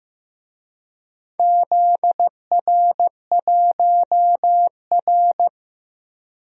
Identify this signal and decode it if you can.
Morse code, heard as a keyed tone, about 15 wpm, 700 Hz. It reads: ZR1R
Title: DXCC-100 Call Signs - 4 Characters - Repeated 3x - 15wpm